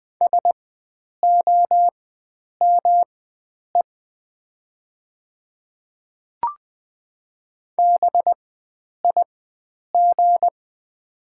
Morse code, heard as a keyed tone, about 20 wpm, spaced as for 9 wpm, 700 Hz. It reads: SOME BIG
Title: 4x Character Spacing - Top 300 Words - 20wpm